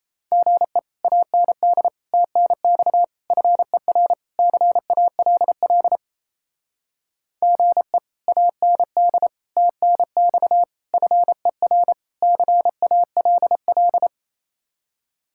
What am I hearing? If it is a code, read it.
Morse code, heard as a keyed tone, about 36 wpm, 700 Hz. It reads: GE AND TNX FER CALL GE AND TNX FER CALL